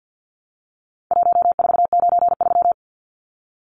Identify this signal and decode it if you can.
Morse code, heard as a keyed tone, about 50 wpm, 700 Hz. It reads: 1493
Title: Numbers - Contest Style - 4-Digit 50wpm